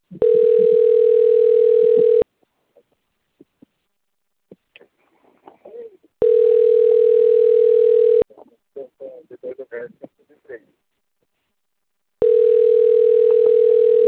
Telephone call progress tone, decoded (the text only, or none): ringback tone